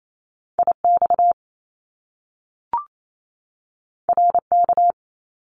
Morse code, heard as a keyed tone, about 28 wpm, 700 Hz. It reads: IX RK
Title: Top Two Letter Combinations in Words 28wpm